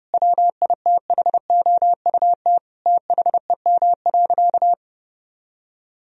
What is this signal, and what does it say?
Morse code, heard as a keyed tone, about 30 wpm, 700 Hz. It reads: WITHOUT THEM.
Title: Sentences from Top 1000 Words - Review Part 3 - 30wpm